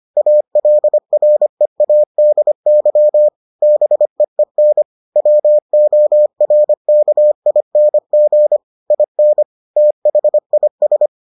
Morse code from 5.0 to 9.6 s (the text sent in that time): WORKING IN